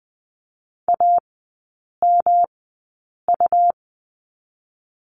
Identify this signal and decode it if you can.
Morse code, heard as a keyed tone, about 19 wpm, 700 Hz. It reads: A M U
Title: 107 Lesson 18e - Letter m - Three Characters 20wpm